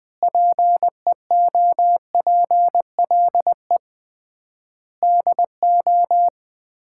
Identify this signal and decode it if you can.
Morse code, heard as a keyed tone, about 21 wpm, 700 Hz. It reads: PEOPLE DO
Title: 3x Word-Spacing - Sets of 5 Words - 20wpm